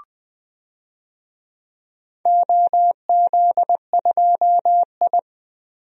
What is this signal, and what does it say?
Morse code, heard as a keyed tone, about 21 wpm, 700 Hz. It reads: OZ2I
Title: DXCC-100 Call Signs - 4 Characters - Repeated 3x - 20wpm